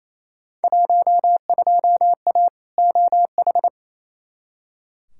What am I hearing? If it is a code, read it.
Morse code, heard as a keyed tone, about 28 wpm, 700 Hz. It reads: 12A OH
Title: Field Day Exchanges Encore - 28wpm